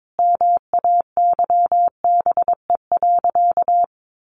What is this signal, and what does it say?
Morse code, heard as a keyed tone, about 22 wpm, 700 Hz. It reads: MAYBE.
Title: Sentences from Top 1000 Words 22wpm